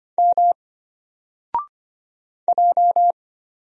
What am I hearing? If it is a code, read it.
Morse code, heard as a keyed tone, about 25 wpm, 700 Hz. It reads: M J
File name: Single-Letter-Number-25wpm